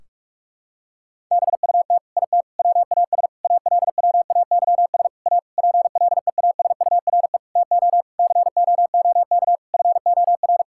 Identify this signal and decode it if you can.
Morse code, heard as a keyed tone, about 41 wpm, 700 Hz. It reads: BUT IT WAS ALWAYS A PLEASURE TO COOK FOR
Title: Sentences from Top 1600 Words 45wpm